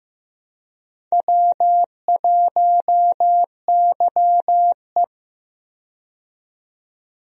Morse code, heard as a keyed tone, about 15 wpm, 700 Hz. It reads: W1YE
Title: USA Call Signs - 4 Characters - 15wpm